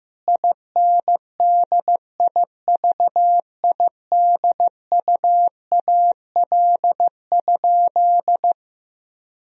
Morse code, heard as a keyed tone, about 15 wpm, 700 Hz. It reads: INDIVIDUAL?
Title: Sentences from Top 1000 Words - Review Part 4 - 15wpm